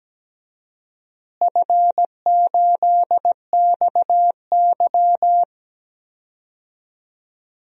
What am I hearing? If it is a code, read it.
Morse code, heard as a keyed tone, about 17 wpm, 700 Hz. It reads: F8XY
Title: DXCC-100 Call Signs - 4 Characters - Repeated 2x - 17wpm